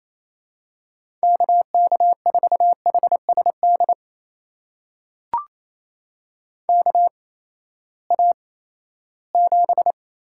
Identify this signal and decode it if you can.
Morse code, heard as a keyed tone, about 28 wpm, spaced as for 16 wpm, 700 Hz. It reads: KK4HSD KA7
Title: USA Call Signs - 6 Characters - 8x Character Spacing - 28wpm